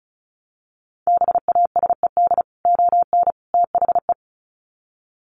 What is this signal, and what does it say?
Morse code, heard as a keyed tone, about 34 wpm, 700 Hz. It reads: BASED ON THE
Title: Sets of 3 Words 35wpm